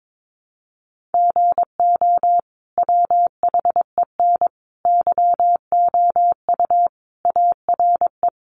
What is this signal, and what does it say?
Morse code, heard as a keyed tone, about 22 wpm, 700 Hz. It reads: GO WHEN YOU ARE
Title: Sentences from Top 500 Words 22wpm